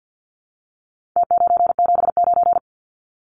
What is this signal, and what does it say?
Morse code, heard as a keyed tone, about 49 wpm, 700 Hz. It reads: T979
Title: Numbers - Contest Style - 4-Digit 50wpm